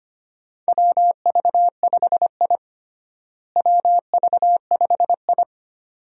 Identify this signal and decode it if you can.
Morse code, heard as a keyed tone, about 25 wpm, 700 Hz. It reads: WV5I WV5I
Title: USA Call Signs - 4 Characters - Repeated 2x - 25wpm